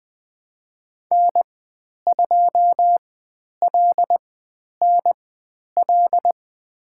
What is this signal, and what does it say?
Morse code, heard as a keyed tone, about 20 wpm, spaced as for 10 wpm, 700 Hz. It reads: N2LNL
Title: 102 Lesson 17g - Letter u - Call Signs 20wpm